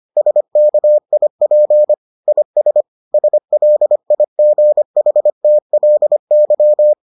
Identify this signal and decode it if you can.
Morse code, heard as a keyed tone, about 25 wpm, 600 Hz. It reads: SKIP IS SLIGHTLY